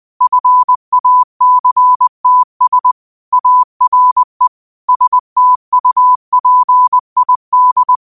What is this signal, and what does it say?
Morse code, heard as a keyed tone, about 20 wpm, 1000 Hz. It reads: FACTS ARE STUPID
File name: qotdfiles / 20wpm